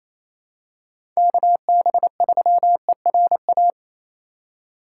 Morse code, heard as a keyed tone, about 28 wpm, 700 Hz. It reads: KB3ERA
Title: USA Call Signs - 2x Character Spacing - 28wpm